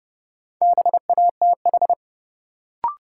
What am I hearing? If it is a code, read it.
Morse code, heard as a keyed tone, about 30 wpm, 700 Hz. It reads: BATH
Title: Sentences from Top 1600 Words - New Word Drill Alt - 30wpm